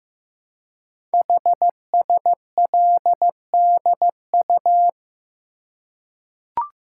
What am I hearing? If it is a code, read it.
Morse code, heard as a keyed tone, about 15 wpm, 700 Hz. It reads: HSLDU
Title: Code Groups - 5 Letter:Numbers - 15wpm